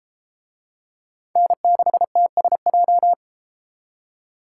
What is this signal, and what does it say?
Morse code, heard as a keyed tone, about 31 wpm, 700 Hz. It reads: N6TSJ